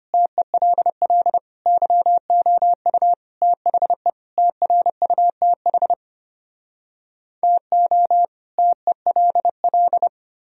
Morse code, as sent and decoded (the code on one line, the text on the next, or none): - . .-.. .-.. / -.-- --- ..- / - .... / - .-. ..- - .... / - --- / - . .-.. .-..
TELL YOU THE TRUTH TO TELL